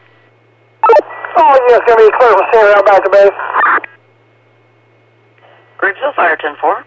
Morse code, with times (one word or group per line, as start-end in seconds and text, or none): none